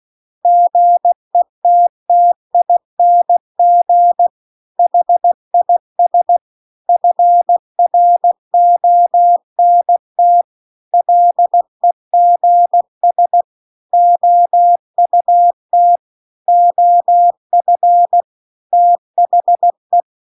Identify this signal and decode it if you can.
Morse code, heard as a keyed tone, about 16 wpm, 700 Hz. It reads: GETTING HIS FRONT LEGS OUT OF THE
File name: CW-The-Big-Twoheaded-River-2-16WPM